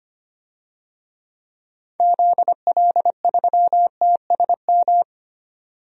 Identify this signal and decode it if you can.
Morse code, heard as a keyed tone, about 25 wpm, 700 Hz. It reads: ZL3TSM
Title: DXCC-100 Call Signs - 6 Characters - 25wpm